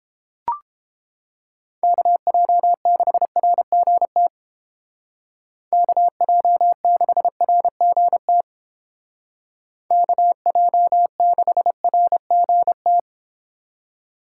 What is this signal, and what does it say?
Morse code, heard as a keyed tone, about 32 wpm, 700 Hz. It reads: KJ6RGT KJ6RGT KJ6RGT